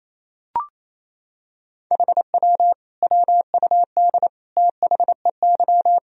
Morse code, heard as a keyed tone, about 28 wpm, 700 Hz. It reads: HW WUD THEY
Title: Sentences from Top 300 Words and Abbreviations 28wpm